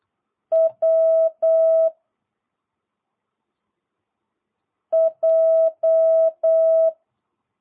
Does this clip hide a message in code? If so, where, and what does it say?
Yes: Morse code from 0.5 to 6.9 s, heard as a keyed tone, about 8 wpm, 660 Hz. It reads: W J